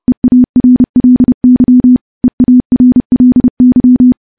Morse code, heard as a keyed tone, about 30 wpm, 250 Hz. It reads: EARLY EARLY
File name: morse / morse-training-30WPM